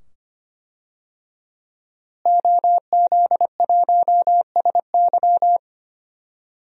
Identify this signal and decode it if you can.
Morse code, heard as a keyed tone, about 25 wpm, 700 Hz. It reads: OZ1SY
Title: DXCC-100 Call Signs - 5 Characters - Repeated 2x - 25wpm